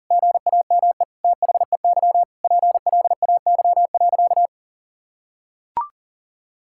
Morse code, heard as a keyed tone, about 40 wpm, 700 Hz. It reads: GAME THEY PLAY.
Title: Sentences from Top 1000 Words - Review Part 5 - 40wpm